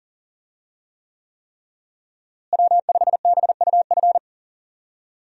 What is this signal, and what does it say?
Morse code, heard as a keyed tone, about 40 wpm, 700 Hz. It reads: W5BUF
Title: USA Call Signs - 5 Characters - Repeated 3x - 40wpm